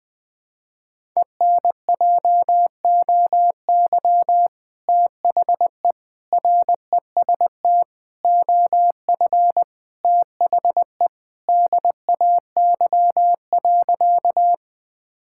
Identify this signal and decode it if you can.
Morse code, heard as a keyed tone, about 20 wpm, 700 Hz. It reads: ENJOY THE REST OF THE DAY.
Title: Sentences with 2-Word Phrases - 20wpm